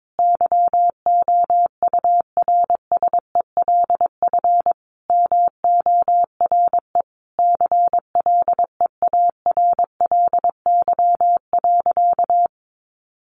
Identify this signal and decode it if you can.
Morse code, heard as a keyed tone, about 22 wpm, 700 Hz. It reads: YOURSELF MORE CLEARLY.
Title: Sentences from Top 1300 Words 22wpm